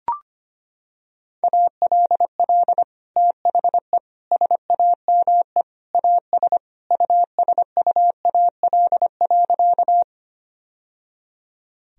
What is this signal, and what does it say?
Morse code, heard as a keyed tone, about 25 wpm, 700 Hz. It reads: ALL THE SAME AS USUAL.
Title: Sentences from Top 1000 Words - Review Part 2 - 25wpm